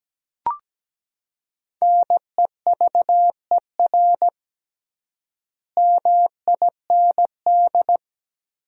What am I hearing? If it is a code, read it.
Morse code, heard as a keyed tone, about 17 wpm, 700 Hz. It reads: NEVER MIND
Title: 3x Word-Spacing - Sentences from Top 500 Words - 17wpm